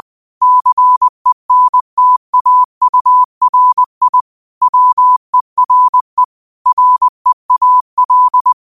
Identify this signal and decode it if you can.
Morse code, heard as a keyed tone, about 20 wpm, 1000 Hz. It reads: CENTAURI WERE REAL
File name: qotdfiles / 20wpm